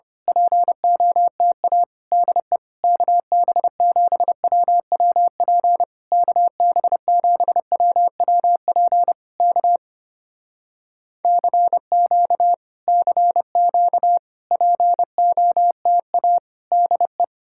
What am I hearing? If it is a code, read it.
Morse code, heard as a keyed tone, about 25 wpm, 700 Hz. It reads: POTA DE KB7WWP KB7WWP K CQ CQ POTA DE
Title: Speed-Racing Common QSO Phrases - 25wpm